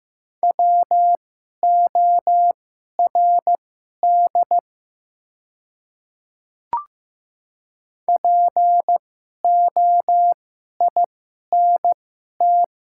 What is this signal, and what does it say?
Morse code, heard as a keyed tone, about 15 wpm, spaced as for 11 wpm, 700 Hz. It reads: WORD POINT
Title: 2x Character Spacing - Top 200 Words - 15wpm